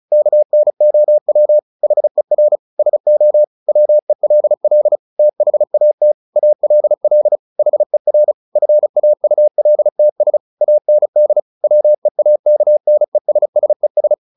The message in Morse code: KNOW HER SO WELL THAT ALL HER FAULTS AND WEAKNESSES

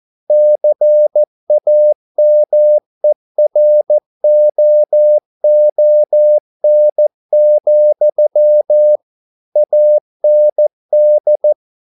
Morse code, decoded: CAMEROON, AND

